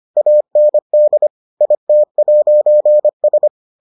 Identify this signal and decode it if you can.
Morse code, heard as a keyed tone, about 25 wpm, 600 Hz. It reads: AND IT'S